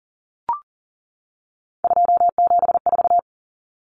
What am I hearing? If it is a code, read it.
Morse code, heard as a keyed tone, about 40 wpm, 700 Hz. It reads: 274
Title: Numbers - 3-Digit 40wpm